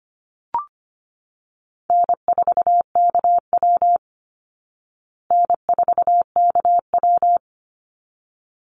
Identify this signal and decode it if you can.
Morse code, heard as a keyed tone, about 25 wpm, 700 Hz. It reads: N4KW N4KW
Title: Call Signs Repeated 3x - 25wpm